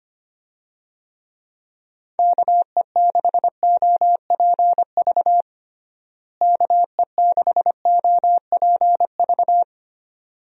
Morse code, heard as a keyed tone, about 25 wpm, 700 Hz. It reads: KE6OPV KE6OPV
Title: USA Call Signs - 6 Characters - Repeated 2x - 25wpm